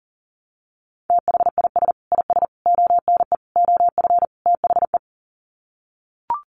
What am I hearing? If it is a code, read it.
Morse code, heard as a keyed tone, about 40 wpm, 700 Hz. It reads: THIS IS ONE OF THE
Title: Sets of 5 Words 40wpm